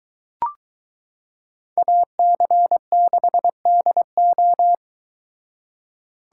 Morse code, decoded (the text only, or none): AC6DO